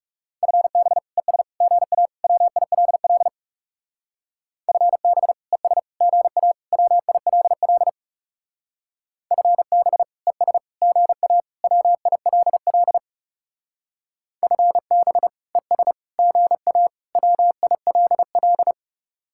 Morse code, heard as a keyed tone, about 39 wpm, 700 Hz. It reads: FB ES GA WILL FB ES GA WILL FB ES GA WILL FB ES GA WILL